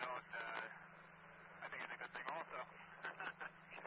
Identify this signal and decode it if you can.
Emergency Alert System (SAME) tones, heard as harsh buzzing bursts, with speech. It is heard from 0.2 to 3.6 s.